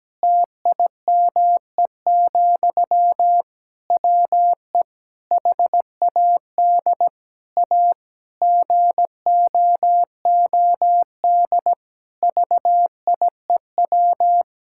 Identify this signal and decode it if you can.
Morse code, heard as a keyed tone, about 17 wpm, 700 Hz. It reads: TIME, WE HAD A GOOD VIEW
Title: Sentences from Top 1000 Words - Review Part 6 - 17wpm